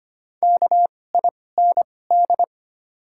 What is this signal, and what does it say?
Morse code, heard as a keyed tone, about 25 wpm, spaced as for 18 wpm, 700 Hz. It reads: KIND
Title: 2x Character Spacing - Top 200 Words - 25wpm